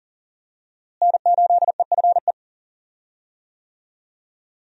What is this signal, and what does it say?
Morse code, heard as a keyed tone, about 40 wpm, 700 Hz. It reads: N8EFE
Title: USA Call Signs - 5 Characters - Repeated 3x - 40wpm